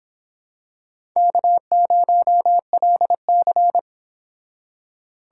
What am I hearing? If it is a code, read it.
Morse code, heard as a keyed tone, about 26 wpm, 700 Hz. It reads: K0LC